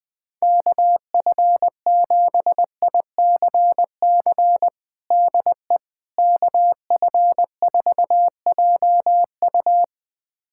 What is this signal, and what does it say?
Morse code, heard as a keyed tone, about 21 wpm, 700 Hz. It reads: KF7ICC DE KF4JU
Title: USA Callsign DE Callsign 20wpm